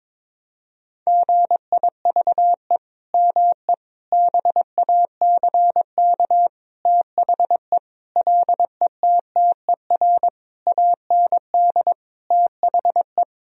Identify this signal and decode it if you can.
Morse code, heard as a keyed tone, about 22 wpm, 700 Hz. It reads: GIVE ME BACK THE LETTER AND THE
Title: Sentences from Top 1000 Words - Review Part 4 - 22wpm